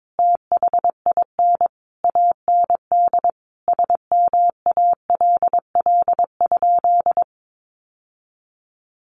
Morse code, heard as a keyed tone, about 22 wpm, 700 Hz. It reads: THIN AND SMALL?
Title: Sentences from Top 1200 Words 22wpm